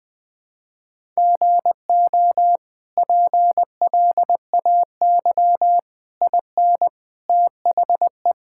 Morse code, heard as a keyed tone, about 21 wpm, 700 Hz. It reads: GO PLAY IN THE
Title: Sentences from Top 1000 Words - Review Part 5 - 20wpm